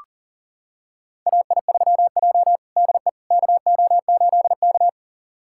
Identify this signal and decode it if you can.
Morse code, heard as a keyed tone, about 40 wpm, 700 Hz. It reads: AI3J DE KO8K
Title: USA Callsign DE Callsign - 4 Characters - 40wpm